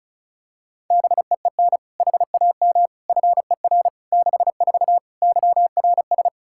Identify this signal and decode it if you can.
Morse code, heard as a keyed tone, about 36 wpm, 700 Hz. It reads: BEEN HAM FER 64 YRS
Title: K1USN SST QSO Practice 35wpm